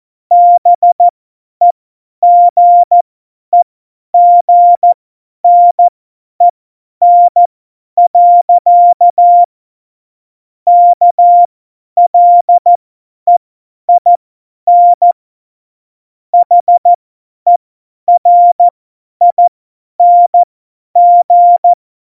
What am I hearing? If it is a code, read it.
Morse code, heard as a keyed tone, about 14 wpm, spaced as for 10 wpm, 700 Hz. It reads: BEGEGNEN. KLEIN HERING
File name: Morsekurs / Witze / Witz 03.Juli 2018, 50 F70